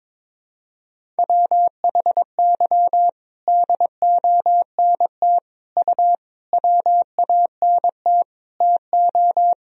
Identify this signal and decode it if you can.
Morse code, heard as a keyed tone, about 21 wpm, 700 Hz. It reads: WHY DONT U WANT TO
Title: Sentences from Top 300 Words and Abbreviations 22wpm